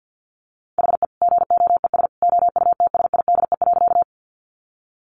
Morse code, heard as a keyed tone, about 50 wpm, 700 Hz. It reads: HE GOES OUTSIDE.
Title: Additional Easy Phrases 50wpm